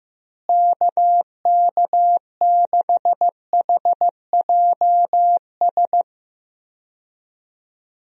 Morse code, heard as a keyed tone, about 16 wpm, 700 Hz. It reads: KK6HJS
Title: USA Call Signs - 6 Characters - Repeated 2x - 15wpm